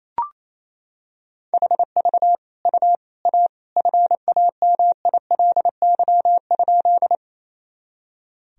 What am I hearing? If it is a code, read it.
Morse code, heard as a keyed tone, about 28 wpm, 700 Hz. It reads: HV U A FAMILY?
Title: Sentences from Top 400 Words and Abbreviations 28wpm